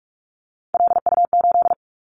Morse code, heard as a keyed tone, about 46 wpm, 700 Hz. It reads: LU8
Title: Callsign Prefixes for Top 50 DXCC - 45wpm